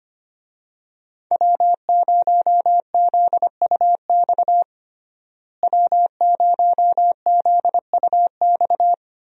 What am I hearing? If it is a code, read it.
Morse code, heard as a keyed tone, about 25 wpm, 690 Hz. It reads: W0ZUX W0ZUX
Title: USA Call Signs - 5 Characters - Repeated 2x - 25wpm